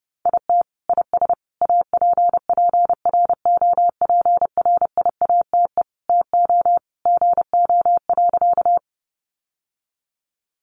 Morse code, heard as a keyed tone, about 28 wpm, 700 Hz. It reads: IT IS APPROPRIATE TO GO.
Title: Sentences from Top 900 Words 30wpm